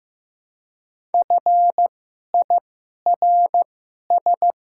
Morse code, heard as a keyed tone, about 15 wpm, spaced as for 11 wpm, 700 Hz. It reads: FIRS
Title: 2x Character Spacing - Top 100 Words - 15wpm